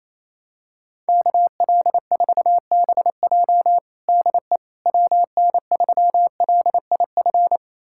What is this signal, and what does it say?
Morse code, heard as a keyed tone, about 28 wpm, 700 Hz. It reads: KL4BJ DE WN3LIF
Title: USA Callsign DE Callsign 28wpm